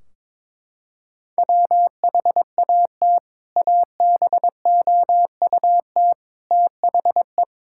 Morse code, heard as a keyed tone, about 22 wpm, 700 Hz. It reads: WHAT ABOUT THE
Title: Sentences from Top 1600 Words 22wpm